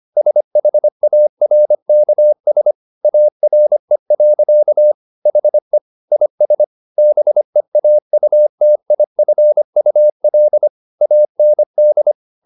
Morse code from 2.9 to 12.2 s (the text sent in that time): ARE. HE IS BEAUTIFUL AND